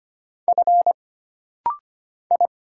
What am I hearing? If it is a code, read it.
Morse code, heard as a keyed tone, about 25 wpm, 700 Hz. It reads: F I
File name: Single Character (Letter:Number:Punctuation:Prosign) - Rapid-Fire - 25wpm